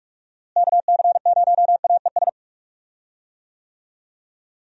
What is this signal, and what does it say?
Morse code, heard as a keyed tone, about 42 wpm, 700 Hz. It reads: KC0AES
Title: USA Call Signs - 6 Characters - Repeated 3x - 45wpm